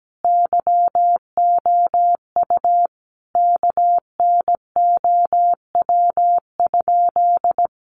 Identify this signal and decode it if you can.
Morse code, heard as a keyed tone, about 17 wpm, 700 Hz. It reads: YOU KNOW?
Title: Sentences from Top 100 Words 17wpm